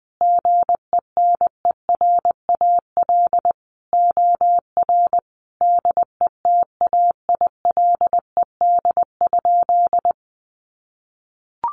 Morse code, heard as a keyed tone, about 20 wpm, 700 Hz. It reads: GENERAL OR DETAILED?
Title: Sentences from Top 1500 Words 20wpm